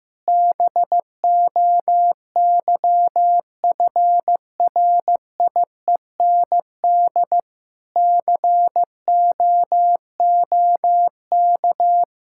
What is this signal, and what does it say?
Morse code, heard as a keyed tone, about 16 wpm, 700 Hz. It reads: BOYFRIEND COOK
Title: Sentences from Top 1000 Words - Review Part 5 - 15wpm